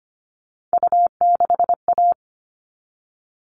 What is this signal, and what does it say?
Morse code, heard as a keyed tone, about 25 wpm, 700 Hz. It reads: U6A
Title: All Cut Numbers - 3-Digit 25wpm